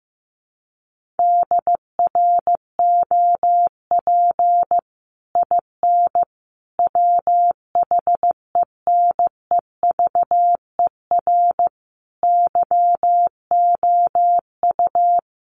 Morse code, heard as a keyed tone, about 15 wpm, 700 Hz. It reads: DROP IN WHENEVER YOU
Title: Sentences from Top 1400 Words 15wpm